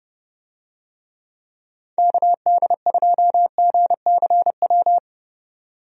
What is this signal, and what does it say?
Morse code, heard as a keyed tone, about 30 wpm, 700 Hz. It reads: KD2GCW
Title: USA Call Signs - 6 Characters - 30wpm